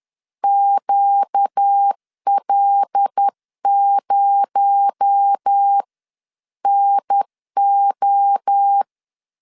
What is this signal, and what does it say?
Morse code, heard as a keyed tone, about 10 wpm, spaced as for 9 wpm, 800 Hz. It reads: QL0NO
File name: CW / DET / 1 9 1500C S A 10wpm 4511